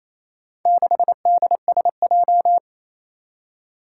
Morse code, heard as a keyed tone, about 28 wpm, 700 Hz. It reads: 6DSJ
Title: Four Challenging Letter-Number Combinations 28wpm